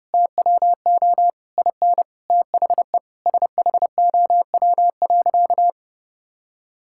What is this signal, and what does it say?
Morse code, heard as a keyed tone, about 30 wpm, 700 Hz. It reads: TWO IN THE SHOW.
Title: Sentences from Top 1000 Words - Review Part 6 - 30wpm